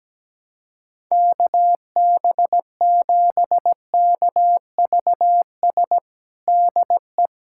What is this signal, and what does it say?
Morse code, heard as a keyed tone, about 17 wpm, 700 Hz. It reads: KB7KVS DE
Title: USA Callsign DE Callsign 17wpm